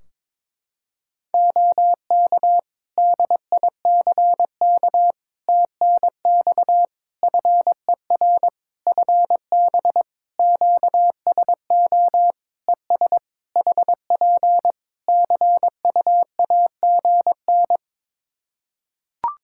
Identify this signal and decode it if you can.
Morse code, heard as a keyed tone, about 22 wpm, 700 Hz. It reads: OK DICK TNX FER FB QSO ES HP CUAGN
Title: Practice QSO - Exchange 4 - Ending - 22wpm